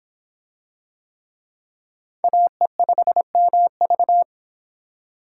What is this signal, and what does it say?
Morse code, heard as a keyed tone, about 23 wpm, 700 Hz. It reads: AE5MV